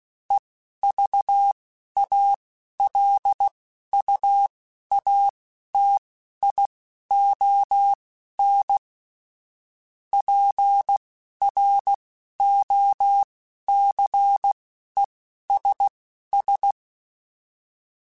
Morse code, heard as a keyed tone, about 16 wpm, spaced as for 11 wpm, 800 Hz. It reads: EVALUATION PROCESS
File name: Technical Subjects / Technical Subjects-mp3s / AND Gate @10